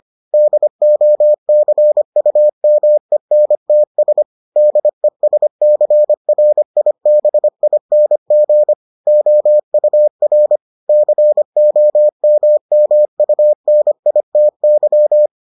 Morse code, heard as a keyed tone, about 25 wpm, 600 Hz. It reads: DOCUMENTS DESCRIBING OUR COMMUNITY